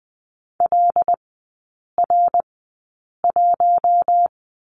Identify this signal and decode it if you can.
Morse code, heard as a keyed tone, about 20 wpm, spaced as for 8 wpm, 700 Hz. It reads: LR1
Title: 114 Lesson 19e - Letter w - Three Characters 20wpm